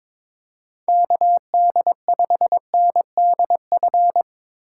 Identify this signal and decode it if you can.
Morse code, heard as a keyed tone, about 22 wpm, 700 Hz. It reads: KD5NDF
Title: USA Call Signs - 6 Characters - 2x Character Spacing - 22wpm